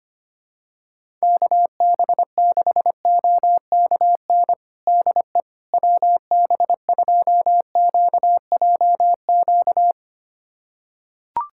Morse code, heard as a keyed tone, about 25 wpm, 700 Hz. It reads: KB6OKN DE WB2QJQ
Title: USA Callsign DE Callsign - 6 Characters - 25wpm